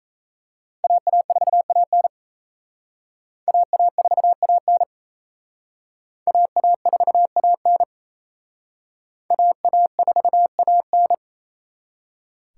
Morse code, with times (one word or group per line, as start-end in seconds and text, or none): 0.8-2.1 AA4AN
3.5-4.8 AA4AN
6.3-7.8 AA4AN
9.3-11.1 AA4AN